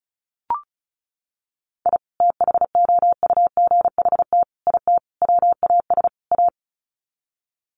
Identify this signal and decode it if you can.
Morse code, heard as a keyed tone, about 34 wpm, 700 Hz. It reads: I THOUGHT IT WAS A
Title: Sets of 5 Words 35wpm